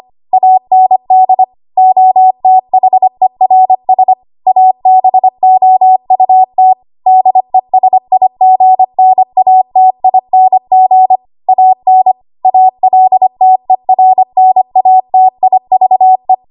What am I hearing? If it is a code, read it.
Morse code, heard as a keyed tone, about 25 wpm, 750 Hz. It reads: AND OTHERS ABOUT DESIGNATING AN ALTERNATIVE
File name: Morse / 25WPM